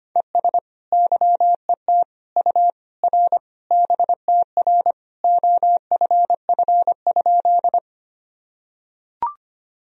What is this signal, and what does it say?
Morse code, heard as a keyed tone, about 25 wpm, 700 Hz. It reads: ES YET U R BTR OFF?
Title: Sentences from Top 500 Words and Abbreviations 25wpm